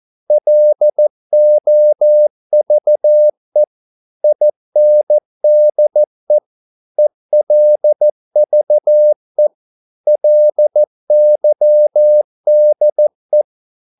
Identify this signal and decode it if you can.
Morse code, heard as a keyed tone, about 14 wpm, 600 Hz. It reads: LOVE INDE ELVE LYDE
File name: Korte ord DK 14wpm